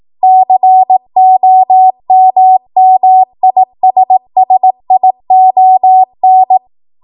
Morse code, heard as a keyed tone, about 18 wpm, 750 Hz. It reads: COMMISSION
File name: Morse / 18WPM